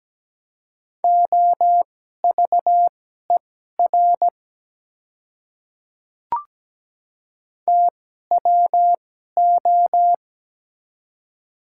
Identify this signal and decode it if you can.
Morse code, heard as a keyed tone, about 17 wpm, spaced as for 12 wpm, 700 Hz. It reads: OVER TWO
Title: 2x Character Spacing - Top 300 Words - 17wpm